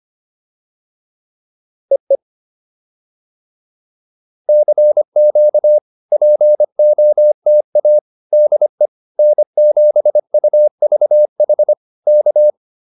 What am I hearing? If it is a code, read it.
Morse code, heard as a keyed tone, about 25 wpm, 600 Hz. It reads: EE CQ POTA DE N7UVH K